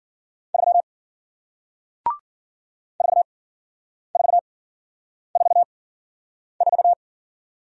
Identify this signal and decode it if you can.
Morse code, heard as a keyed tone, about 53 wpm, spaced as for 9 wpm, 700 Hz. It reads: 3 4444